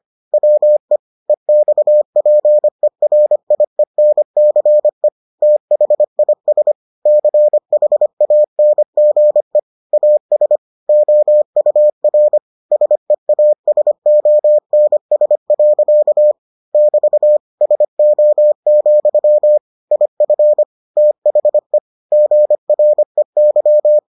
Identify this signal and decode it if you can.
Morse code, heard as a keyed tone, about 25 wpm, 600 Hz. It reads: WE EXPERIENCE THIS CHANGE AS OUR SEASONS. <BT> SO, IF THE GREY